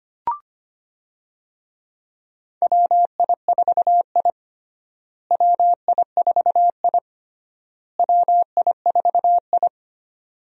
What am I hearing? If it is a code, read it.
Morse code, heard as a keyed tone, about 25 wpm, 700 Hz. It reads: WI4I WI4I WI4I
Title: USA Call Signs - 4 Characters - Repeated 3x - 25wpm